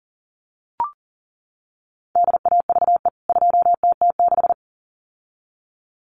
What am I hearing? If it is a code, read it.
Morse code, heard as a keyed tone, about 40 wpm, 700 Hz. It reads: DAVE 2TT6
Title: CWT Exchange 40wpm